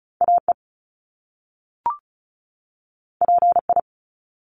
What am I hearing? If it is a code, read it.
Morse code, heard as a keyed tone, about 36 wpm, 700 Hz. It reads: AE PI
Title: Top Two Letter Combinations in Words 35wpm